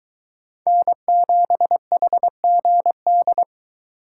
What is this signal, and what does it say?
Morse code, heard as a keyed tone, about 23 wpm, 700 Hz. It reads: N7HGD